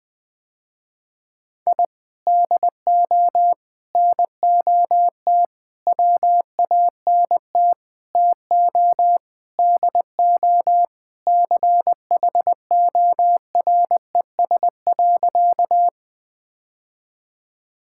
Morse code, heard as a keyed tone, about 20 wpm, 700 Hz. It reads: I DO NOT WANT TO DO CHORES.
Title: Sentences with 5-Word Phrases - Spoken Beginning - 20wpm